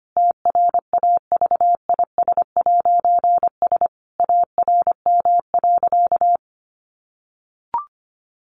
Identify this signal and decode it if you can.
Morse code, heard as a keyed tone, about 25 wpm, 700 Hz. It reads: TRAVIS'S ARM.
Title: A Sound Of Thunder 25wpm